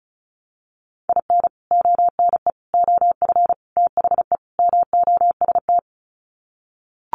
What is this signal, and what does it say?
Morse code, heard as a keyed tone, about 34 wpm, 700 Hz. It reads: IN ONE OF THE MOST
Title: Sets of 5 Words Encore 35wpm